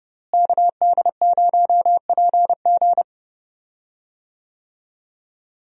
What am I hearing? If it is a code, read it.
Morse code, heard as a keyed tone, about 29 wpm, 700 Hz. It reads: KD0PG
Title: USA Call Signs - 5 Characters - Repeated 3x - 30wpm